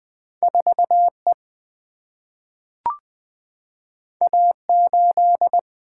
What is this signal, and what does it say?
Morse code, heard as a keyed tone, about 20 wpm, 700 Hz. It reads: 4E A8
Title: Single-Letter Single-Number 20wpm